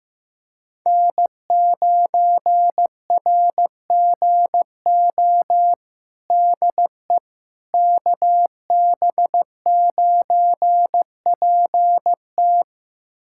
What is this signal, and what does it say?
Morse code, heard as a keyed tone, about 15 wpm, 700 Hz. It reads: N9RGO DE KB9PT
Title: USA Callsign DE Callsign - 5 Characters - 15wpm